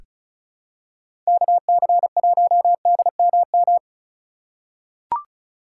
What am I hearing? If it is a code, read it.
Morse code, heard as a keyed tone, about 36 wpm, 700 Hz. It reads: KC1DMM
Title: USA Call Signs - 6 Characters - 4x Character Spacing - 35wpm